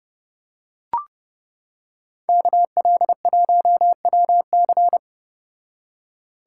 Morse code, heard as a keyed tone, about 31 wpm, 700 Hz. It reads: KL1WC